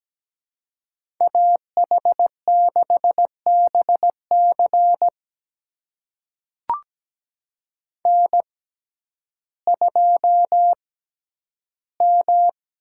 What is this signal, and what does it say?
Morse code, heard as a keyed tone, about 17 wpm, spaced as for 10 wpm, 700 Hz. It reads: AH6BC N2M
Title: USA Call Signs - 5 Characters - 6x Character Spacing - 17wpm